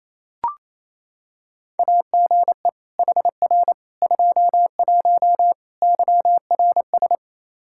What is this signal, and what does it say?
Morse code, heard as a keyed tone, about 27 wpm, 700 Hz. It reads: AGE HR 21 YRS
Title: K1USN SST QSO Practice 28wpm